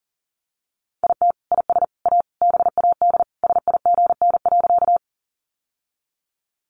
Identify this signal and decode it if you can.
Morse code, heard as a keyed tone, about 37 wpm, 700 Hz. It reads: IT IS A BAD SIGN.
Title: Sentences from Top 1000 Words 40wpm